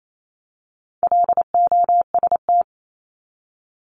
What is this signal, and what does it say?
Morse code, heard as a keyed tone, about 28 wpm, 700 Hz. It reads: LOST
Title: Top 1251-1500 Words 28wpm